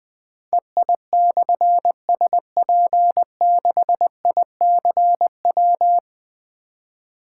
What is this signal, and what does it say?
Morse code, heard as a keyed tone, about 20 wpm, 700 Hz. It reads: EI/SP6ICW
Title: Difficult Call Signs 20wpm